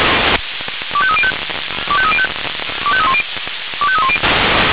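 Five-tone selective call (ZVEI), heard as quick runs of five tones, tones of about 70 ms. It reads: 25305 25305 25310 25310
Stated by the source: ZVEI five-tone call received by SDR.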